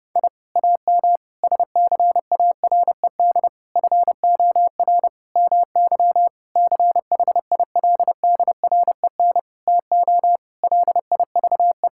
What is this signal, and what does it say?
Morse code, heard as a keyed tone, about 32 wpm, 700 Hz. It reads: I AM SCARED FOR MY CHILDREN TO LIVE
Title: Sentences from Top 1000 Words - Review Part 5 - 30wpm